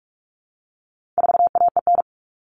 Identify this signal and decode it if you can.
Morse code, heard as a keyed tone, about 44 wpm, 700 Hz. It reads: 4AEN
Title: All Cut Numbers - 4-Digit 45wpm